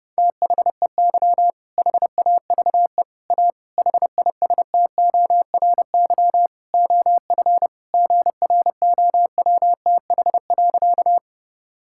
Morse code, heard as a keyed tone, about 32 wpm, 700 Hz. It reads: THEY HAVE A HISTORY OF GROWTH.
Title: Sentences from Top 1000 Words - Review Part 5 - 30wpm